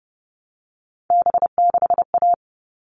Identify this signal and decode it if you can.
Morse code, heard as a keyed tone, about 30 wpm, 700 Hz. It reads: B6A
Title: All Cut Numbers - 3-Digit 30wpm